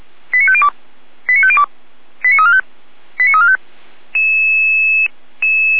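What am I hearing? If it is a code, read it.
Five-tone selective call (ZVEI), heard as quick runs of five tones, tones of about 70 ms. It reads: 89502 89502 89345 89345
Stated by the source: ZVEI five-tone selective call received by SDR, served by feuerwehr-orlamuende.org fire brigade.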